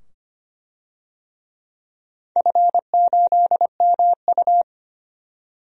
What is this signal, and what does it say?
Morse code, heard as a keyed tone, about 25 wpm, 700 Hz. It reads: F8MU
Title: DXCC-100 Call Signs - 4 Characters - 25wpm